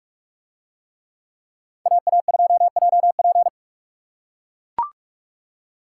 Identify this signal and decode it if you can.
Morse code, heard as a keyed tone, about 44 wpm, 700 Hz. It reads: AA2JP